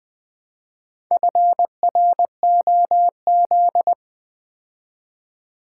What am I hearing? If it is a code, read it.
Morse code, heard as a keyed tone, about 20 wpm, 700 Hz. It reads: FROZ
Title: Build A Word - 2 Letters At A Time - 6-Letter Words - 20wpm